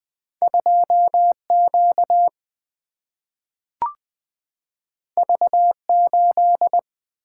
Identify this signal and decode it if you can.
Morse code, heard as a keyed tone, about 20 wpm, 700 Hz. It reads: 2Q V8
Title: Single-Letter Single-Number 20wpm